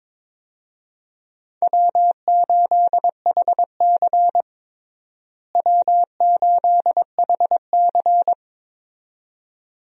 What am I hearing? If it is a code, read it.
Morse code, heard as a keyed tone, about 22 wpm, 700 Hz. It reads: W8HC W8HC
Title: USA Call Signs - 4 Characters - Repeated 2x - 22wpm